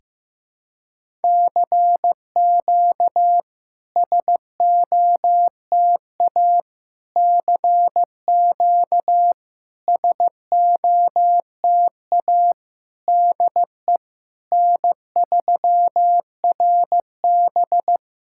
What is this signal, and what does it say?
Morse code, heard as a keyed tone, about 15 wpm, 700 Hz. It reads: CQ SOTA CQ SOTA DE N3RB
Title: SOTA QSO Practice 15wpm